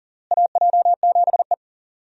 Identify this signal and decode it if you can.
Morse code, heard as a keyed tone, about 39 wpm, 700 Hz. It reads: AJ7E
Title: USA Call Signs - 4 Characters - Repeated 2x - 40wpm